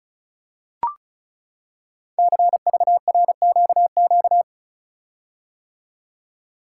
Morse code, heard as a keyed tone, about 36 wpm, 690 Hz. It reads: CVRQQ
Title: Code Groups - 5 Letters - 35wpm